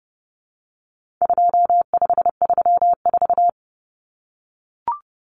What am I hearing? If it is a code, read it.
Morse code, heard as a keyed tone, about 30 wpm, 700 Hz. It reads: 2534
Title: Numbers - 4-Digit 30wpm